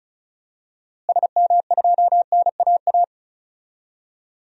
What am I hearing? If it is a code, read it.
Morse code, heard as a keyed tone, about 34 wpm, 700 Hz. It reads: SM2NAA
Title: DXCC-100 Call Signs - 6 Characters - Repeated 3x - 35wpm